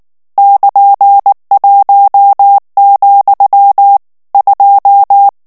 Morse code, heard as a keyed tone, about 19 wpm, 800 Hz. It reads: <KN>1, 2
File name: morse mp3 / ITALIEN texte clair 2 vitesse 18 mots minute